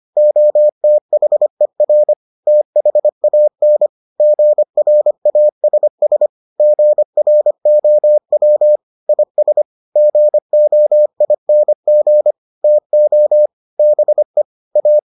Morse code, heard as a keyed tone, about 25 wpm, 600 Hz. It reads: OTHER THAN GRASS GROW IS GOING TO BE A